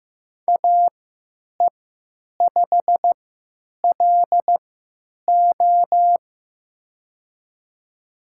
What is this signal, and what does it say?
Morse code, heard as a keyed tone, about 15 wpm, spaced as for 8 wpm, 700 Hz. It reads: AE5LO
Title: USA Call Signs - 3x Character Spacing - 15wpm